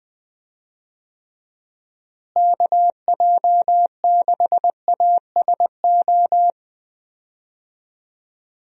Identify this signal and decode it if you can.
Morse code, heard as a keyed tone, about 21 wpm, 700 Hz. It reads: KJ6ASO
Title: USA Call Signs - 6 Characters - Repeated 3x - 20wpm